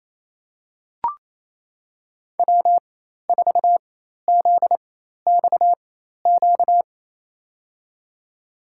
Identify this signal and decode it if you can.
Morse code, heard as a keyed tone, about 28 wpm, spaced as for 13 wpm, 700 Hz. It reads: W4ZXQ
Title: USA Call Signs - 5 Characters - 4x Character Spacing - 28wpm